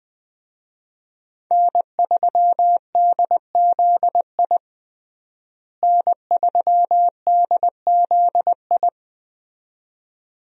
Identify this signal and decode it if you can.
Morse code, heard as a keyed tone, about 20 wpm, 700 Hz. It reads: N3DZI N3DZI
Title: USA Call Signs - 5 Characters - Repeated 2x - 20wpm